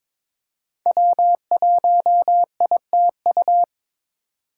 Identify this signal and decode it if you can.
Morse code, heard as a keyed tone, about 22 wpm, 700 Hz. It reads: W1ITU
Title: USA Call Signs - 5 Characters - 6x Character Spacing - 22wpm